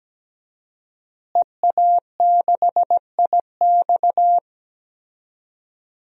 Morse code, heard as a keyed tone, about 17 wpm, 700 Hz. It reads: EA6IX
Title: DXCC-100 Call Signs - 5 Characters - Repeated 3x - 17wpm